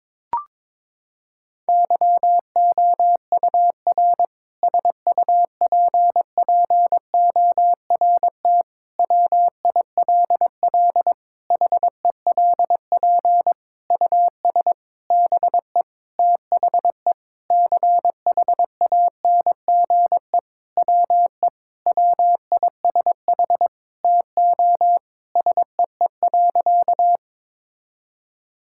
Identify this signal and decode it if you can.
Morse code, heard as a keyed tone, about 22 wpm, 700 Hz. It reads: YOUR SUPPORT WILL HELP US BE THE CHANGE WE WISH TO SEE.
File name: Sentences with 7-Word Phrases - 22wpm